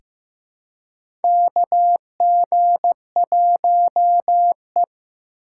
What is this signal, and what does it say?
Morse code, heard as a keyed tone, about 15 wpm, 700 Hz. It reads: KG1E
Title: USA Call Signs - 3x Character Spacing - 15wpm